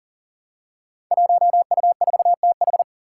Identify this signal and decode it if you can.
Morse code, heard as a keyed tone, about 42 wpm, 700 Hz. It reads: 1U4TH
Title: Five Challenging Letter-Number Combinations 40wpm